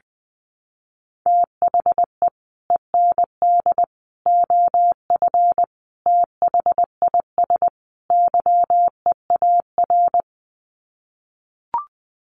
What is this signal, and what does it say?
Morse code, heard as a keyed tone, about 20 wpm, 700 Hz. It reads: THE END OF THIS YEAR
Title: Sets of 5 Words Encore 20wpm